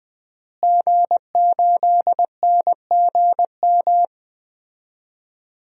Morse code, heard as a keyed tone, about 20 wpm, 700 Hz. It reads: G8NGM
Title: DXCC-100 Call Signs - 5 Characters - Repeated 3x - 20wpm